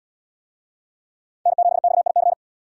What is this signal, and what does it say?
Morse code, heard as a keyed tone, about 72 wpm, 700 Hz. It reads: N7XIC